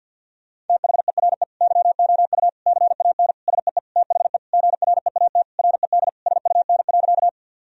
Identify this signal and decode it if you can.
Morse code, heard as a keyed tone, about 50 wpm, 700 Hz. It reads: THERE YOU CAN SEE THE GREAT RED SUN.
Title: Sentences from Top 1000 Words - Review Part 2 - 50wpm